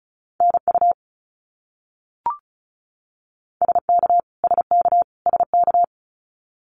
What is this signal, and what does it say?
Morse code, heard as a keyed tone, about 34 wpm, 700 Hz. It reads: NU SK SK SK
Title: Canadian Provinces and Territories - Repeated 3x - 35wpm